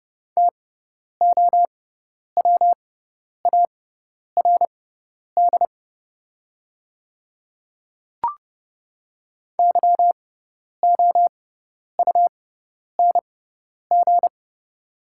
Morse code, heard as a keyed tone, about 30 wpm, spaced as for 10 wpm, 700 Hz. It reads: TOWARD YOUNG
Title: 6x Character Spacing - Top 500 Words - 30wpm